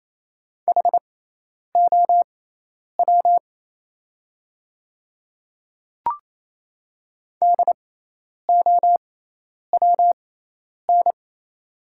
Morse code, heard as a keyed tone, about 28 wpm, spaced as for 10 wpm, 700 Hz. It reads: HOW DOWN
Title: 6x Character Spacing - Top 200 Words - 28wpm